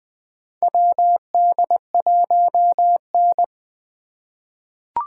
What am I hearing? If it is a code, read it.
Morse code, heard as a keyed tone, about 21 wpm, 700 Hz. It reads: WD1N
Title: USA Call Signs - 4 Characters - 4x Character Spacing - 20wpm